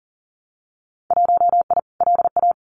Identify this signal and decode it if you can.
Morse code, heard as a keyed tone, about 38 wpm, 700 Hz. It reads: 1I LA
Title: Winter Field Day Exchanges Encore 40wpm